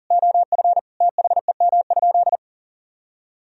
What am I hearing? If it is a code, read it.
Morse code, heard as a keyed tone, about 40 wpm, 700 Hz. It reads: OF THEM?
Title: Sentences from Top 1000 Words - Review Part 4 - 40wpm